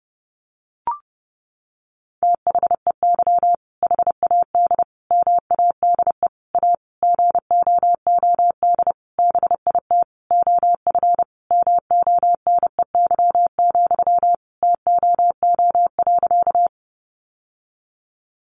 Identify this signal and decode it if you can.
Morse code, heard as a keyed tone, about 30 wpm, 700 Hz. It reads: THEY HAD MADE A GOOD BIT OF MONEY, TOO.
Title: Sentences from Top 900 Words 30wpm